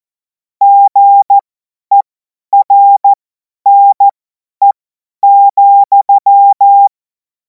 Morse code, heard as a keyed tone, about 14 wpm, spaced as for 10 wpm, 800 Hz. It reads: GERNE,
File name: Morsekurs / Witze / Witz 04. August 2026 50 F70